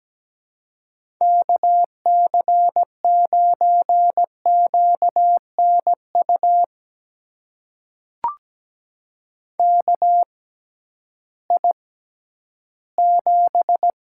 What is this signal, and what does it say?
Morse code, heard as a keyed tone, about 17 wpm, spaced as for 11 wpm, 700 Hz. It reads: KC9QNU KI7
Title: USA Call Signs - 6 Characters - 6x Character Spacing - 17wpm